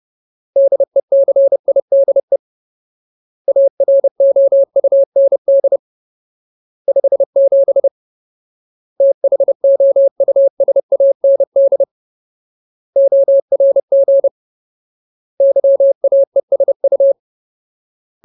Morse code, heard as a keyed tone, about 30 wpm, 550 Hz. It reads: DECIDE AROUND 57 THOUSAND ORG YAESU
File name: Ultimate Word List / 30WPM Ultimate